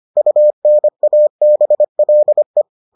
Morse code, heard as a keyed tone, about 25 wpm, 600 Hz. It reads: UNABLE